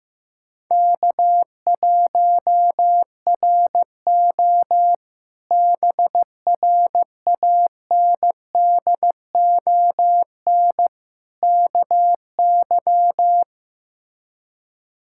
Morse code, heard as a keyed tone, about 15 wpm, 700 Hz. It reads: K1RO BRANDON KY
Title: K1USN SST QSO Practice - 15wpm